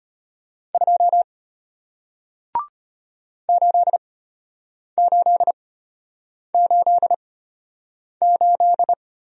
Morse code, heard as a keyed tone, about 34 wpm, spaced as for 8 wpm, 700 Hz. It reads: 2 8888